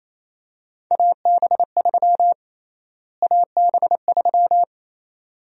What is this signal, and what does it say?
Morse code, heard as a keyed tone, about 28 wpm, 700 Hz. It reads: AB3 AB3
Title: USA Call Sign Prefixes - Repeated 2x - 28wpm